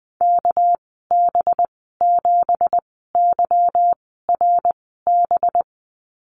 Morse code, heard as a keyed tone, about 20 wpm, spaced as for 14 wpm, 700 Hz. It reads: KB7YRB
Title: 238 Lesson 38g - Letter q - Call Signs 20wpm